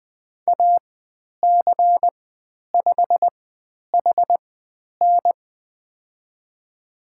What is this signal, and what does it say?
Morse code, heard as a keyed tone, about 20 wpm, spaced as for 10 wpm, 700 Hz. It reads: AC5HN
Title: 109 Lesson 18g - Letter m - Call Signs 20wpm